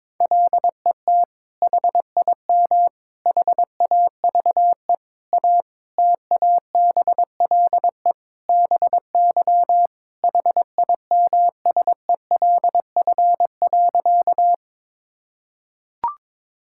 Morse code, heard as a keyed tone, about 22 wpm, 700 Hz. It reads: LET HIM HAVE A TABLE BY HIMSELF.
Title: Sentences from Top 1600 Words 22wpm